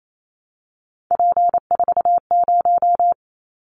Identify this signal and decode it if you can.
Morse code, heard as a keyed tone, about 28 wpm, 700 Hz. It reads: P40
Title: Callsign Prefixes for Top 100 DXCC - 28wpm